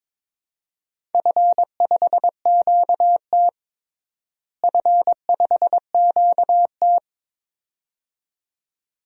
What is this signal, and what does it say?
Morse code, heard as a keyed tone, about 22 wpm, 700 Hz. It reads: F5QT F5QT
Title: DXCC-100 Call Signs - 4 Characters - Repeated 2x - 22wpm